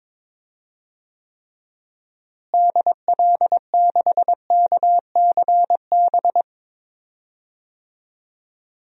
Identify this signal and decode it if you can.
Morse code, heard as a keyed tone, about 23 wpm, 700 Hz. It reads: DL6KCB
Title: DXCC-100 Call Signs - 6 Characters - Repeated 3x - 22wpm